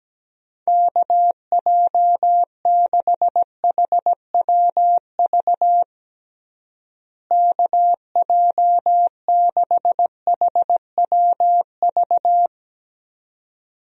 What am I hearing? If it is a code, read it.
Morse code, heard as a keyed tone, about 17 wpm, 700 Hz. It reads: KJ6HWV KJ6HWV
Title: USA Call Signs - 6 Characters - Repeated 2x - 17wpm